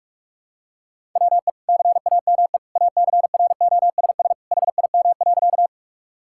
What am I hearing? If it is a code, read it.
Morse code, heard as a keyed tone, about 44 wpm, 700 Hz. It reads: WE CAME ACROSS HIM.
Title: Sentences from Top 1000 Words - Review Part 5 - 45wpm